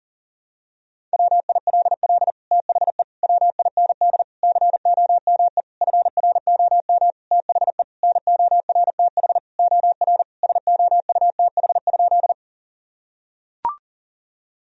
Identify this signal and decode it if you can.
Morse code, heard as a keyed tone, about 40 wpm, 700 Hz. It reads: WILL THE WIND COME FROM THE NORTH OR SOUTH?
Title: Sentences from Top 1000 Words - Review Part 2 - 40wpm